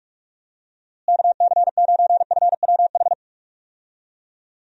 Morse code, heard as a keyed tone, about 48 wpm, 700 Hz. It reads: KC9FWH
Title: USA Call Signs - 6 Characters - Repeated 2x - 45wpm